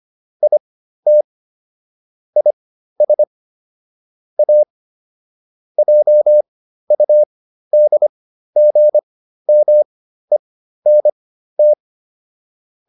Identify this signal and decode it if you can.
Morse code, heard as a keyed tone, about 25 wpm, spaced as for 13 wpm, 600 Hz. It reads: IT IS A JUDGMENT